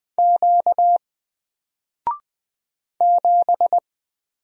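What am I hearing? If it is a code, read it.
Morse code, heard as a keyed tone, about 20 wpm, 700 Hz. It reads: Q 7
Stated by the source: Single letter and number Morse drill at 20 wpm, from N4LCD.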